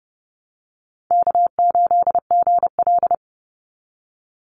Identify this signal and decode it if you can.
Morse code, heard as a keyed tone, about 30 wpm, 700 Hz. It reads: K8GL
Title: Easy Call Signs Repeated 2x - 30wpm